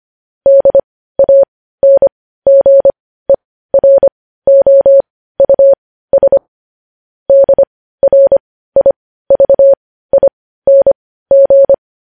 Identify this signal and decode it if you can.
Morse code, heard as a keyed tone, about 25 wpm, spaced as for 15 wpm, 550 Hz. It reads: DANGEROUS DRIVING
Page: news headlines in morse code at 15 wpm